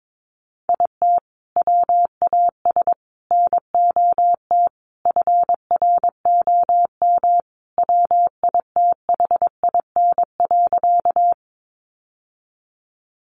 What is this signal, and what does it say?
Morse code, heard as a keyed tone, about 22 wpm, 700 Hz. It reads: IT WAS NOT FROM WITHIN.
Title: Sentences from Top 700 Words 22wpm